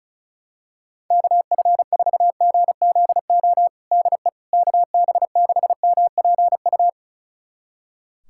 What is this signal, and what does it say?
Morse code, heard as a keyed tone, about 34 wpm, 700 Hz. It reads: KF4GZO DE KB6MPU
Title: USA Callsign DE Callsign - 6 Characters - 35wpm